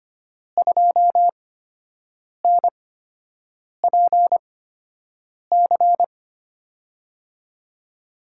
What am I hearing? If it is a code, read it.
Morse code, heard as a keyed tone, about 25 wpm, spaced as for 7 wpm, 700 Hz. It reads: 2NPC
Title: USA Call Signs - 6 Characters - 8x Character Spacing - 25wpm